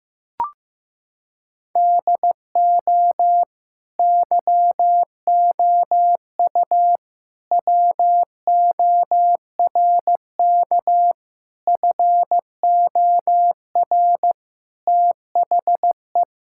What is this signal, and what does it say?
Morse code, heard as a keyed tone, about 15 wpm, 700 Hz. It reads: DO YOU WORK FOR THE
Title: Sentences from Top 1000 Words - Review Part 5 - 15wpm